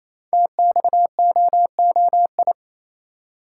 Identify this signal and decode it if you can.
Morse code, heard as a keyed tone, about 27 wpm, 700 Hz. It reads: TXOOI
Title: Code Groups - 5 Letters - 28wpm